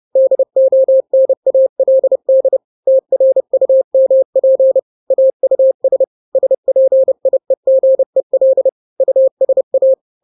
Morse code, heard as a keyed tone, about 30 wpm, 530 Hz. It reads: DONALD TRUMP AUS SPIEGEL USA
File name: daily / 29wpm